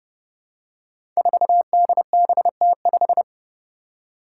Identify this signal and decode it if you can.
Morse code, heard as a keyed tone, about 32 wpm, 700 Hz. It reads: 4DBT5
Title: Five Challenging Letter-Number Combinations 30wpm